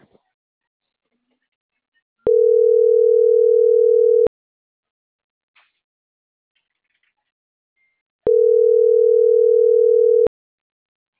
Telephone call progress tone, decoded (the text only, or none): ringback tone